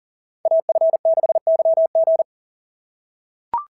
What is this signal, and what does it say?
Morse code, heard as a keyed tone, about 38 wpm, 690 Hz. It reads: AF6YG
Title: SOTA QSO Practice 40wpm